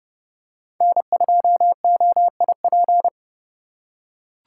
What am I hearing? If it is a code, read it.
Morse code, heard as a keyed tone, about 30 wpm, 700 Hz. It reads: N2OIP
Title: USA Call Signs - 5 Characters - 4x Character Spacing - 30wpm